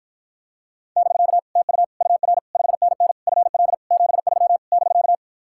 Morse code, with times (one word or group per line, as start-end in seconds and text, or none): 1.0-1.4 <BK>
1.6-1.8 TU
2.0-2.4 UR
2.5-3.1 5NN
3.3-3.7 FL
3.9-4.6 73
4.7-5.2 <BK>